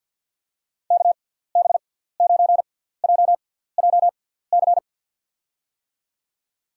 Morse code, heard as a keyed tone, about 54 wpm, spaced as for 17 wpm, 700 Hz. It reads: KB9JJC
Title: USA Call Signs - 6 Characters - 6x Character Spacing - 50wpm